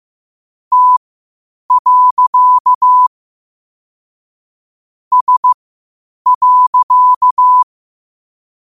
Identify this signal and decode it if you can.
Morse code, heard as a keyed tone, about 15 wpm, spaced as for 8 wpm, 1000 Hz. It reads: T. S.
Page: Quotes for Thu, 14 Aug 2025 in Morse Code at 8 words per minute.